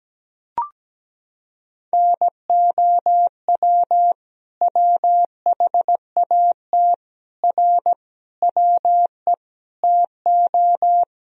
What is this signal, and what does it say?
Morse code, heard as a keyed tone, about 17 wpm, 700 Hz. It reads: NOW WHAT R WE TO
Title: Sentences from Top 100 Words and Abbreviations 17wpm